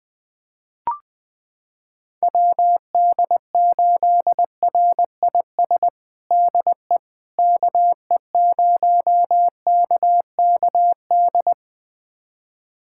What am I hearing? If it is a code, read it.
Morse code, heard as a keyed tone, about 20 wpm, 700 Hz. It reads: WD8RIS DE KE0KKD
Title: USA Callsign DE Callsign - 6 Characters - 20wpm